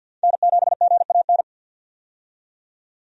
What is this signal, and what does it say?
Morse code, heard as a keyed tone, about 51 wpm, 700 Hz. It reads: N7GAN
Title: USA Call Signs - 5 Characters - 50wpm